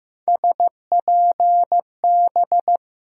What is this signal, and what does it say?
Morse code, heard as a keyed tone, about 15 wpm, 700 Hz. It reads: SPB
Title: USA Call Sign Postfixes - 3 Letters 15wpm